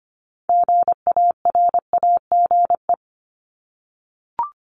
Morse code, heard as a keyed tone, about 25 wpm, 700 Hz. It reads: GARAGE
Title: Top 1501-1750 Words 25wpm